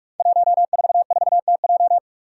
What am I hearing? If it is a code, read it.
Morse code, heard as a keyed tone, about 44 wpm, 700 Hz. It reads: 144TJ
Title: Five Challenging Letter-Number Combinations 45wpm